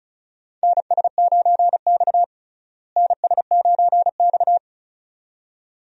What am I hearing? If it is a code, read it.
Morse code, heard as a keyed tone, about 34 wpm, 700 Hz. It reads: NS9X NS9X
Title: USA Call Signs - 4 Characters - Repeated 2x - 35wpm